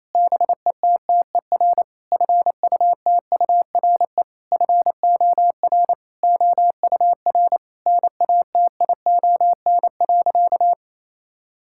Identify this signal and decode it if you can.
Morse code, heard as a keyed tone, about 28 wpm, 700 Hz. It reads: BETTER FUTURE FOR OUR NATION.
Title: Sentences from Top 1000 Words - Review Part 6 - 28wpm